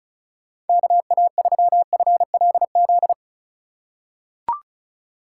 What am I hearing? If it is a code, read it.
Morse code, heard as a keyed tone, about 34 wpm, 700 Hz. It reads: KA3FLZ
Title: USA Call Signs - 4x Character Spacing - 35wpm